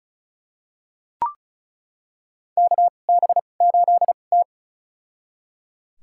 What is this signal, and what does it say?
Morse code, heard as a keyed tone, about 36 wpm, spaced as for 25 wpm, 690 Hz. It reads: KB8T
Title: USA Call Signs - 4 Characters - 2x Character Spacing - 35wpm